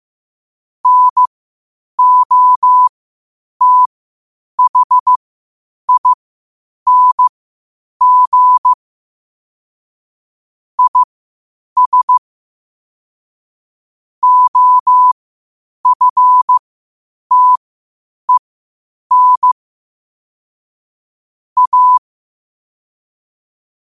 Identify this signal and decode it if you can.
Morse code, heard as a keyed tone, about 15 wpm, spaced as for 8 wpm, 1000 Hz. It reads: NOTHING IS OFTEN A
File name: qotdfiles / 8wpm